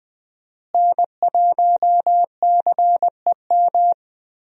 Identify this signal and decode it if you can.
Morse code, heard as a keyed tone, about 20 wpm, 700 Hz. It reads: N1CEM
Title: USA Call Signs - 5 Characters - Repeated 2x - 20wpm